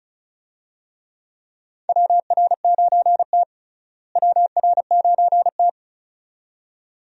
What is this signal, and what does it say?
Morse code, heard as a keyed tone, about 34 wpm, 700 Hz. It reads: WR9T WR9T
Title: USA Call Signs - 4 Characters - Repeated 2x - 35wpm